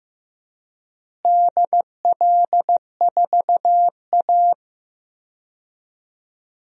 Morse code, heard as a keyed tone, about 15 wpm, 700 Hz. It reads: DL4A
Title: DXCC-100 Call Signs - 4 Characters - Repeated 2x - 15wpm